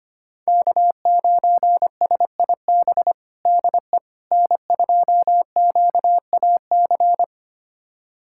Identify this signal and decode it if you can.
Morse code, heard as a keyed tone, about 25 wpm, 700 Hz. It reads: K9SIB DE N2QAC
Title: USA Callsign DE Callsign - 5 Characters - 25wpm